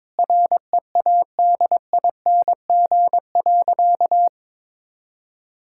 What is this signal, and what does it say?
Morse code, heard as a keyed tone, about 22 wpm, 700 Hz. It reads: READING.
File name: Difficult Sentences with Hints - 22wpm